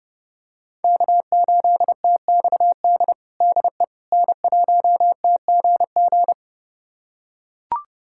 Morse code, heard as a keyed tone, about 32 wpm, 700 Hz. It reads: K8TXD DE N1TGG
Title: USA Callsign DE Callsign - 5 Characters - 30wpm